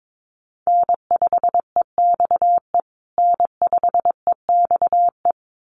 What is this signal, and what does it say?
Morse code, heard as a keyed tone, about 22 wpm, 700 Hz. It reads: N5EXE N5EXE
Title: USA Call Signs Repeated 2x 22wpm